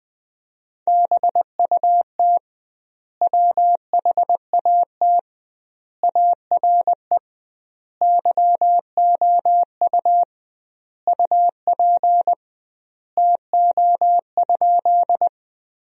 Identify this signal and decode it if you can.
Morse code, heard as a keyed tone, about 21 wpm, 700 Hz. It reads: BUT WHAT ARE YOU UP TO?
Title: 2x Word-Spacing - Sentences from Top 100 Words - 20wpm